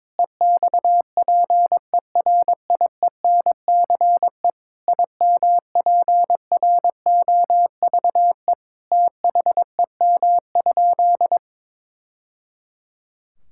Morse code, heard as a keyed tone, about 22 wpm, 700 Hz. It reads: EXPERIENCE IMPROVE THEM?
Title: Sentences from Top 1000 Words - Review Part 5 - 22wpm